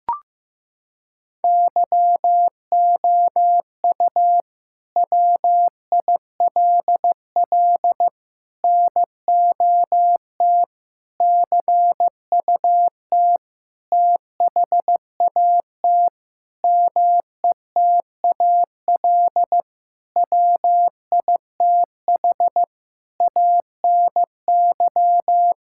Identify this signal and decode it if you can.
Morse code, heard as a keyed tone, about 16 wpm, 700 Hz. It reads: YOU WILL NOT CUT THAT METAL WITH ANY
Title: Sentences from Top 1000 Words - Review Part 6 - 15wpm